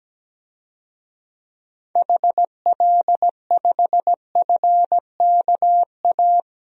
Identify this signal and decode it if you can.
Morse code, heard as a keyed tone, about 17 wpm, 700 Hz. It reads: HL5FKA
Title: DXCC-100 Call Signs - 6 Characters - 17wpm